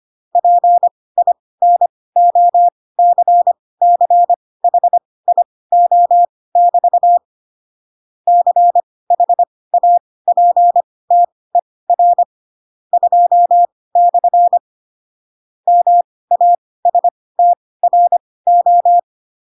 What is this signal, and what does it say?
Morse code, heard as a keyed tone, about 25 wpm, spaced as for 18 wpm, 700 Hz. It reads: PINOCCHIO<BT> CHAPTER 2/ MASTRO
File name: Pinocchio / Pinocchio ch2 12 25 ik1hge